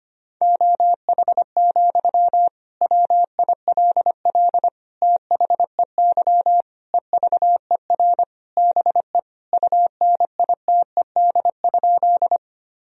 Morse code, heard as a keyed tone, about 25 wpm, 700 Hz. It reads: OH, WILL THEY EVER BE UNITED?
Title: Sentences from Top 1000 Words - Review Part 4 - 25wpm